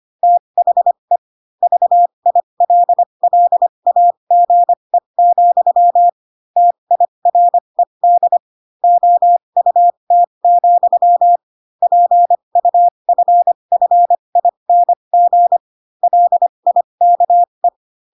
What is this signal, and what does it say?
Morse code, heard as a keyed tone, about 25 wpm, spaced as for 22 wpm, 700 Hz. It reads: THE VILLAGE, TIRED OUT, PUFFING LIKE
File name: Pinocchio / Pinocchio ch6 18 25 ik1hge